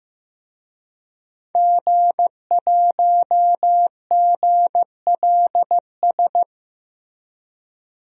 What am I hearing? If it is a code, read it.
Morse code, heard as a keyed tone, about 15 wpm, 700 Hz. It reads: G1GLS
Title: DXCC-100 Call Signs - 5 Characters - Repeated 2x - 15wpm